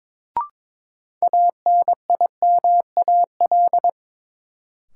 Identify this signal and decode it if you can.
Morse code, heard as a keyed tone, about 21 wpm, 700 Hz. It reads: ANIMAL
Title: Sentences from Top 1000 Words - Review Part 1 - New Word Drill - 22wpm